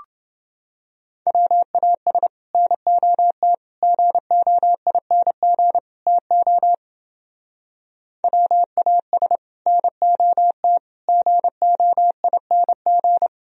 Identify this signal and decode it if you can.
Morse code, heard as a keyed tone, about 29 wpm, 700 Hz. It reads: WAS NOT GOING TO WAS NOT GOING